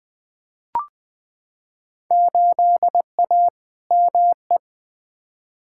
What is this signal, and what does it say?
Morse code, heard as a keyed tone, about 20 wpm, 700 Hz. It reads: 8A ME
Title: Field Day Exchanges - 20wpm